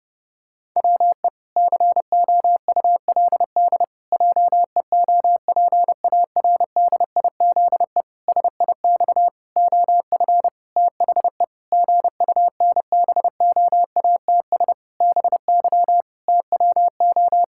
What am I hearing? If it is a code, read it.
Morse code, heard as a keyed tone, about 30 wpm, 700 Hz. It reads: WE COULD JEOPARDIZE SIX OF THE GUNBOATS BY TWO
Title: Pangram Challenge v2 - 30wpm